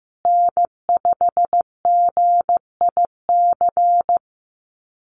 Morse code, heard as a keyed tone, about 14 wpm, 700 Hz. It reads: N5GIC
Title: USA Call Signs Repeated 2x 15wpm